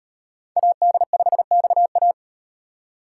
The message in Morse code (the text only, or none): AD5XA